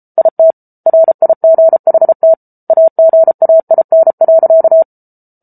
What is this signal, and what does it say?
Morse code, heard as a keyed tone, about 34 wpm, 650 Hz. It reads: IT RIGHT AGAIN.